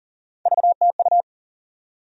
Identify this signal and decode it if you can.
Morse code, heard as a keyed tone, about 40 wpm, 700 Hz. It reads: VTU
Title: USA Call Sign Postfixes - 3 Letters 40wpm